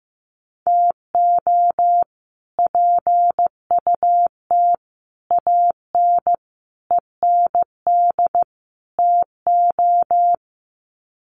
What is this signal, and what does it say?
Morse code, heard as a keyed tone, about 15 wpm, 700 Hz. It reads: TO PUT AN END TO
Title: Sets of 5 Words Encore2 15wpm